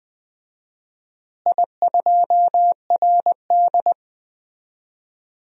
Morse code, heard as a keyed tone, about 20 wpm, 700 Hz. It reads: I2RD
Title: DXCC-100 Call Signs - 4 Characters - Repeated 3x - 20wpm